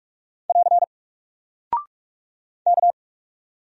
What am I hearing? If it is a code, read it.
Morse code, heard as a keyed tone, about 44 wpm, 700 Hz. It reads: <AR> K
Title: Single Character (Letter:Number:Punctuation:Prosign) Alt 45wpm